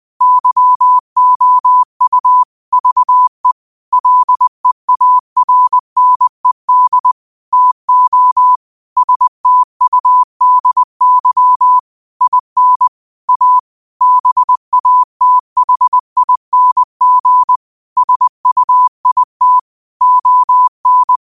Morse code, heard as a keyed tone, about 20 wpm, 1000 Hz. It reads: YOU VE LEARNED TO STUDY IN A BATHING SUIT ON
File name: qotdfiles / 20wpm